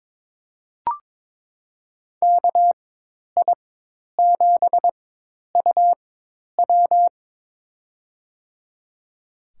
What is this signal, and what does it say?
Morse code, heard as a keyed tone, about 22 wpm, spaced as for 10 wpm, 700 Hz. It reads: KI7UW
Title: USA Call Signs - 5 Characters - 4x Character Spacing - 22wpm